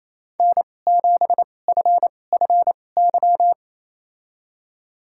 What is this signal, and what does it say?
Morse code, heard as a keyed tone, about 28 wpm, spaced as for 20 wpm, 700 Hz. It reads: N7FFY
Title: USA Call Signs - 2x Character Spacing - 28wpm